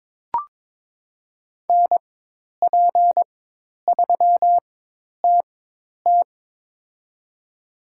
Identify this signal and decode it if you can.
Morse code, heard as a keyed tone, about 22 wpm, spaced as for 10 wpm, 700 Hz. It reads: NP3TT
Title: USA Call Signs - 5 Characters - 4x Character Spacing - 22wpm